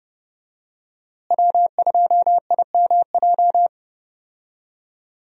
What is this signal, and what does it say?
Morse code, heard as a keyed tone, about 30 wpm, 700 Hz. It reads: W2IMJ
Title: USA Call Signs - 5 Characters - Repeated 3x - 30wpm